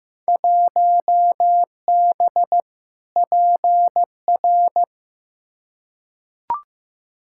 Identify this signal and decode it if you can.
Morse code, heard as a keyed tone, about 16 wpm, 700 Hz. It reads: 1B PR
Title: Field Day Exchanges Encore - 15wpm